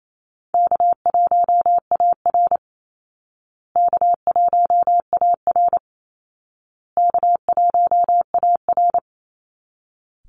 Morse code, heard as a keyed tone, about 28 wpm, 700 Hz. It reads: K1AR K1AR K1AR
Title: Easy Call Signs Repeated 3x - 28wpm